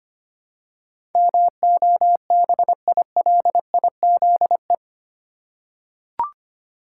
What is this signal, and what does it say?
Morse code, heard as a keyed tone, about 25 wpm, 700 Hz. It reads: MOBILIZE
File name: Eight Letter Words - 25wpm